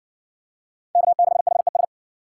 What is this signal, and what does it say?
Morse code, heard as a keyed tone, about 60 wpm, 700 Hz. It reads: K6HES